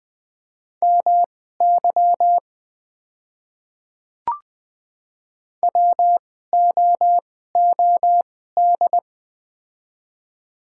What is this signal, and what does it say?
Morse code, heard as a keyed tone, about 20 wpm, spaced as for 14 wpm, 700 Hz. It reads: MY WOOD
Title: 2x Character Spacing - Top 300 Words - 20wpm